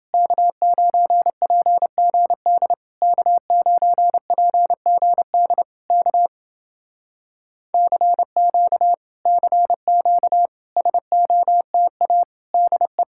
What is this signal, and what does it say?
Morse code, heard as a keyed tone, about 31 wpm, 700 Hz. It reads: K9PGD K9PGD K CQ CQ SOTA DE